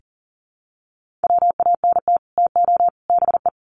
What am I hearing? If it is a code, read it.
Morse code, heard as a keyed tone, about 40 wpm, 700 Hz. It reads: WANT TO BE
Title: Sets of 3 Words 40wpm